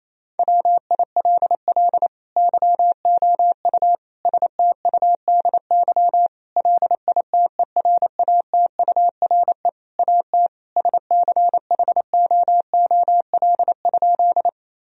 Morse code, heard as a keyed tone, about 28 wpm, 700 Hz. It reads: WILL YOU STUDY LITERATURE AT SCHOOL?
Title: Sentences from Top 1000 Words - Review Part 3 - 28wpm